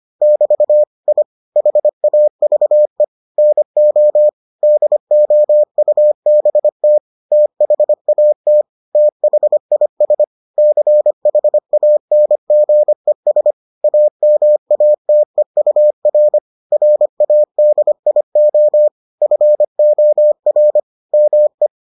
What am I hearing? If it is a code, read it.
Morse code, heard as a keyed tone, about 25 wpm, 600 Hz. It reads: <BT> I HAVE NO DOUBT THAT THIS CHANGES AMATEUR RADIO FOR ME